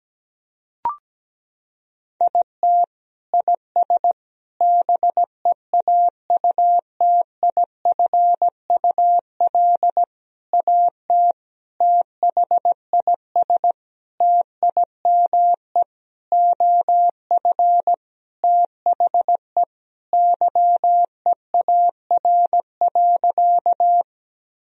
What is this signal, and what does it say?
Morse code, heard as a keyed tone, about 17 wpm, 700 Hz. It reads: IT IS BEAUTIFUL AT THIS TIME OF THE YEAR.
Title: Sentences from Top 1000 Words - Review Part 1 - 17wpm